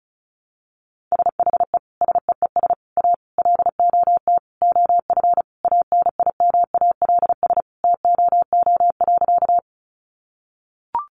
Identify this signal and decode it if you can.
Morse code, heard as a keyed tone, about 34 wpm, 700 Hz. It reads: SHE SEES A LOT OF ANIMALS TOO.
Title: Easy Phrases 35wpm